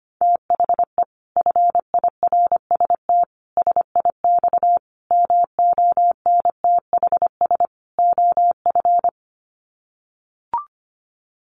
Morse code, heard as a keyed tone, about 25 wpm, 700 Hz. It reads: THE FIRST SIX MONTHS OF
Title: Sets of 5 Words Encore2 25wpm